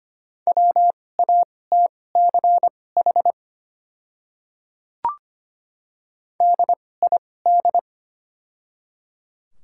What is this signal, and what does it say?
Morse code, heard as a keyed tone, about 25 wpm, spaced as for 18 wpm, 700 Hz. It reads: WATCH DID
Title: 2x Character Spacing - Top 300 Words - 25wpm